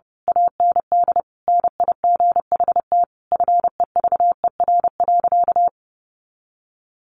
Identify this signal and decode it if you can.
Morse code, heard as a keyed tone, about 30 wpm, 700 Hz. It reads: AND NIGHT FEVER.
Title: A Sound Of Thunder 30wpm